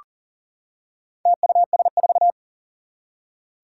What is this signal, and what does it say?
Morse code, heard as a keyed tone, about 38 wpm, 700 Hz. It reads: TUS4
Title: Four Challenging Letter-Number Combinations 40wpm